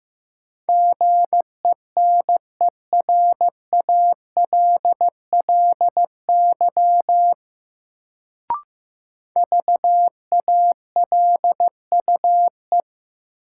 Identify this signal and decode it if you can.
Morse code, heard as a keyed tone, about 15 wpm, 700 Hz. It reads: GENERALLY VALUE
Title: Sentences from Top 600 Words - New Word Drill - 15wpm